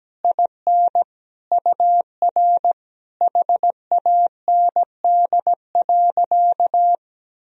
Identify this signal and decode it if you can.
Morse code, heard as a keyed tone, about 17 wpm, 700 Hz. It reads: IN UR HAND.
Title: Sentences from Top 400 Words and Abbreviations 17wpm